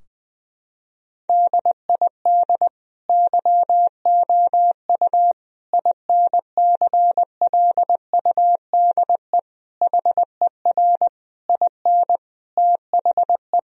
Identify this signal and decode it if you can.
Morse code, heard as a keyed tone, about 20 wpm, 700 Hz. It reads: DID YOU INCLUDE HER IN THE
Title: Sentences from Top 1000 Words - Review Part 3 - 20wpm